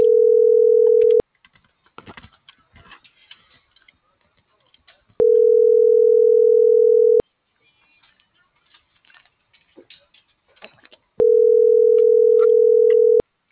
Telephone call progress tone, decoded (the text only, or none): ringback tone